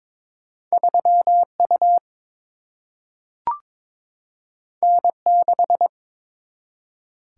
Morse code, heard as a keyed tone, about 22 wpm, 700 Hz. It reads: 3U N6
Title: Single-Letter Single-Number 22wpm